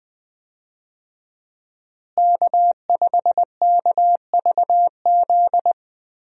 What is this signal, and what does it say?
Morse code, heard as a keyed tone, about 20 wpm, 700 Hz. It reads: K5KVZ
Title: USA Call Signs - 5 Characters - Repeated 3x - 20wpm